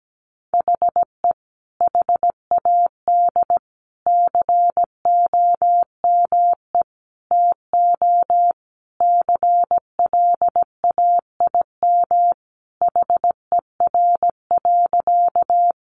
Morse code, heard as a keyed tone, about 17 wpm, 700 Hz. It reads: HE HAD COME TO CLAIM HER.
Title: Sentences from Top 1500 Words 17wpm